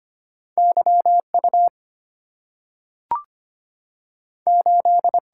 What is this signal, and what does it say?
Morse code, heard as a keyed tone, about 25 wpm, 700 Hz. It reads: Y U 8
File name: Two Letter-Number Combinations - 25wpm